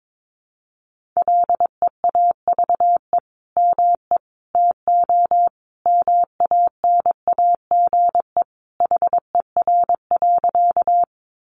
Morse code, heard as a keyed tone, about 22 wpm, 700 Hz. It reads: LEAVE ME TO MANAGE HER.
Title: Sentences from Top 1400 Words 22wpm